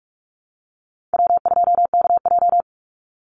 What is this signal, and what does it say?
Morse code, heard as a keyed tone, about 44 wpm, 700 Hz. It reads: W2KJ
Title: Easy Call Signs - 45wpm